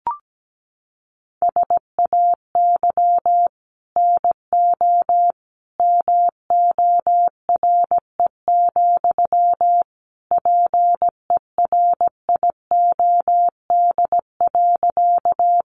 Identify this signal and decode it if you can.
Morse code, heard as a keyed tone, about 17 wpm, 700 Hz. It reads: SAY NO MORE, PERIOD.
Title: Sentences from Top 800 Words 17wpm